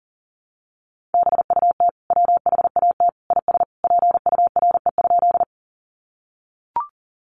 Morse code, heard as a keyed tone, about 40 wpm, 700 Hz. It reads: BUT WHAT IS PURE?
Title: Sentences from Top 1500 Words 40wpm